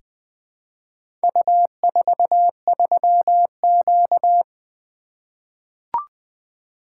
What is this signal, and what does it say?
Morse code, heard as a keyed tone, about 20 wpm, 700 Hz. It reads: U43Q
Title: Four Letter-Number Combinations - 20wpm